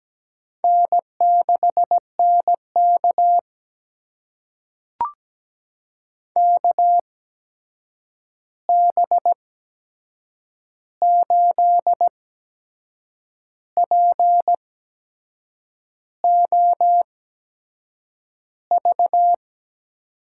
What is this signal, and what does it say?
Morse code, heard as a keyed tone, about 17 wpm, spaced as for 6 wpm, 700 Hz. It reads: N6NK KB8POV
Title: USA Call Signs - 8x Character Spacing - 17wpm